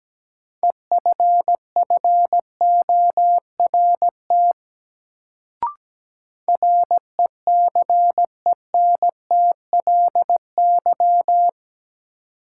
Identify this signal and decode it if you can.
Morse code, heard as a keyed tone, about 17 wpm, 700 Hz. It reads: EFFORT RECENTLY
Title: Sentences from Top 1200 Words - New Word Drill Alt - 17wpm